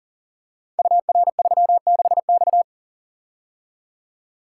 Morse code, heard as a keyed tone, about 38 wpm, 700 Hz. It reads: UR36X
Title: Code Groups - 5 Letter:Numbers - 40wpm